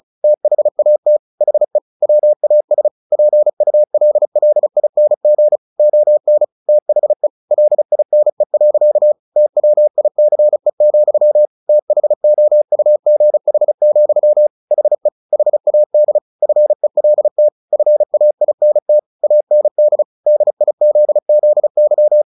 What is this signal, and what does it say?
Morse code, heard as a keyed tone, about 34 wpm, 600 Hz. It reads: THAT HE WAS PULLING ON THE LINE. TWICE, THOUGH, HE HAD FELT FAINT AND DIZZY
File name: the old man and the sea 2 35wpm